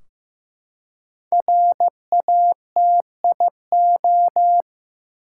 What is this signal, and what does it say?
Morse code, heard as a keyed tone, about 15 wpm, 700 Hz. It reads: RATIO
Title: Five Letter Words - 15wpm